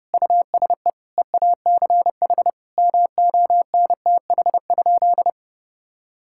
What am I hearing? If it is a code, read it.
Morse code, heard as a keyed tone, about 30 wpm, 700 Hz. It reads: USE EACH MONTH?
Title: Sentences from Top 1000 Words - Review Part 6 - 30wpm